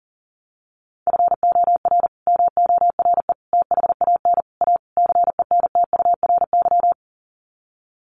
Morse code, heard as a keyed tone, about 40 wpm, 700 Hz. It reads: FOR MORE THAN A CENTURY
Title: Sets of 5 Words Encore2 40wpm